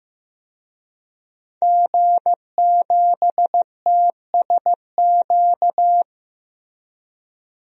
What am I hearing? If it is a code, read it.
Morse code, heard as a keyed tone, about 15 wpm, 700 Hz. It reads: G7TSQ
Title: DXCC-100 Call Signs - 5 Characters - 15wpm